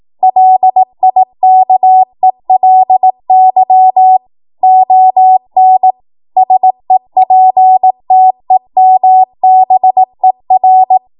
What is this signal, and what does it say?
Morse code, heard as a keyed tone, about 18 wpm, 750 Hz. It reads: LIKELY ON SEPTEMBER
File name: Morse / 18WPM